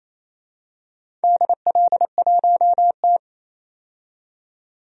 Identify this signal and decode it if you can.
Morse code, heard as a keyed tone, about 28 wpm, 700 Hz. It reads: DL1T
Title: DXCC-100 Call Signs - 4 Characters - 28wpm